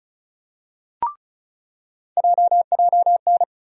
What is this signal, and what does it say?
Morse code, heard as a keyed tone, about 36 wpm, 700 Hz. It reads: JJN
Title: USA Call Sign Postfixes - 3 Letters 35wpm